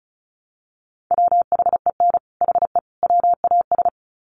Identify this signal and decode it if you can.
Morse code, heard as a keyed tone, about 34 wpm, 700 Hz. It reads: WHEN HE WAS
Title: Sets of 3 Words 35wpm